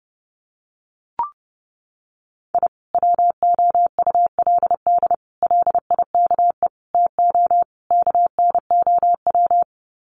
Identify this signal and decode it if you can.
Morse code, heard as a keyed tone, about 30 wpm, 700 Hz. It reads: I WOULD LIKE TO KNOW
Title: Sets of 5 Words Encore 30wpm